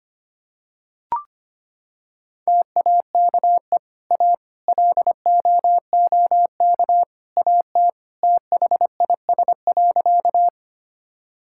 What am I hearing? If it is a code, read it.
Morse code, heard as a keyed tone, about 25 wpm, 700 Hz. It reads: TAKE A LOOK AT THIS.
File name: practice / 25wpm / Sentences with 3-Word Phrases - 25wpm